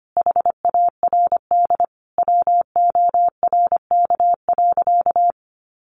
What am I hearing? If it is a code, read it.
Morse code, heard as a keyed tone, about 25 wpm, 700 Hz. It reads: HARD WORK.
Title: Sentences from Top 1300 Words 25wpm